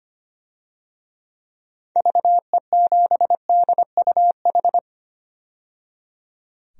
Morse code, heard as a keyed tone, about 25 wpm, 700 Hz. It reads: VE7DUH
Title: DXCC-100 Call Signs - 6 Characters - 25wpm